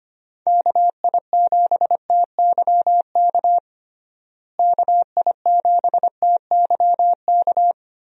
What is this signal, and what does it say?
Morse code, heard as a keyed tone, about 25 wpm, 700 Hz. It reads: KI7TYK KI7TYK
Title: USA Call Signs - 6 Characters - Repeated 2x - 25wpm